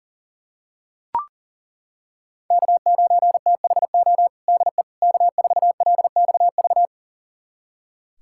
Morse code, heard as a keyed tone, about 40 wpm, 700 Hz. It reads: K9THO DE K4LXV
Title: USA Callsign DE Callsign 40wpm